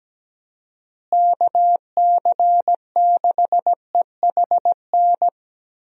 Morse code, heard as a keyed tone, about 17 wpm, 700 Hz. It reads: KC6EHN
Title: USA Call Signs - 6 Characters - Repeated 3x - 17wpm